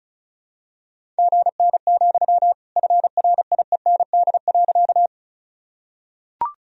Morse code, heard as a keyed tone, about 34 wpm, 700 Hz. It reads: GN, FRIEND.
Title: Sentences from Top 300 Words and Abbreviations 35wpm